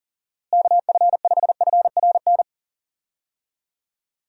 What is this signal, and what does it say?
Morse code, heard as a keyed tone, about 37 wpm, 700 Hz. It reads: KF5FRN
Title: USA Call Signs - 6 Characters - Repeated 2x - 40wpm